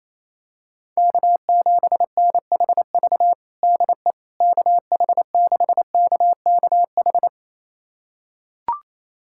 Morse code, heard as a keyed tone, about 28 wpm, 700 Hz. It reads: K7NHV DE KH6KKH
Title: USA Callsign DE Callsign 28wpm